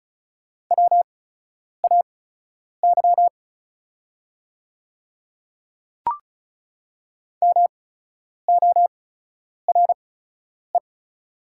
Morse code, heard as a keyed tone, about 36 wpm, spaced as for 10 wpm, 700 Hz. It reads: WAY MORE
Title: 8x Character Spacing - Top 100 Words - 35wpm